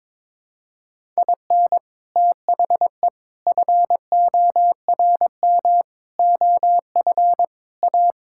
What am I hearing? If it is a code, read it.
Morse code, heard as a keyed tone, about 22 wpm, 700 Hz. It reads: IN THE FORM OF A
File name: practice / 22wpm / Sentences with 4-Word Phrases - 22wpm